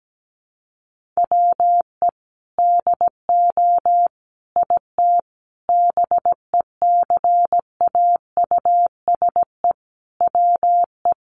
Morse code, heard as a keyed tone, about 17 wpm, 700 Hz. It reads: WE DO IT BECAUSE WE
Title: Sentences from Top 1400 Words 17wpm